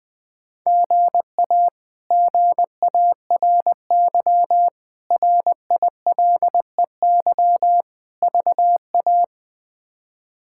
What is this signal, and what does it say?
Morse code, heard as a keyed tone, about 20 wpm, 700 Hz. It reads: GA GARY RILEY VA
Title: K1USN SST QSO Practice - 20wpm